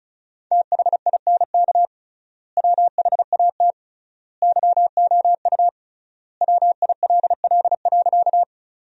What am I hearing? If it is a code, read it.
Morse code, heard as a keyed tone, about 34 wpm, 700 Hz. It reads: THINK WHAT YOU WILL.
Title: 3x Word-Spacing - Sentences from Top 200 Words - 35wpm